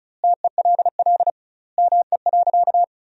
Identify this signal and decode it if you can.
Morse code, heard as a keyed tone, about 34 wpm, 700 Hz. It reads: TELL ME.
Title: 2x Word-Spacing - Sentences from Top 500 Words - 35wpm